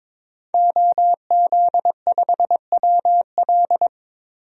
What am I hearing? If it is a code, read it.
Morse code, heard as a keyed tone, about 22 wpm, 700 Hz. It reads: OZ5WL
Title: DXCC-100 Call Signs - 5 Characters - 22wpm